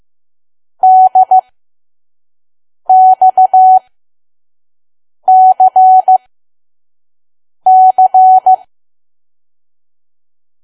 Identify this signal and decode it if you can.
Morse code, heard as a keyed tone, about 15 wpm, spaced as for 5 wpm, 750 Hz. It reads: DXCC
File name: Archive / 5 WPM files / PM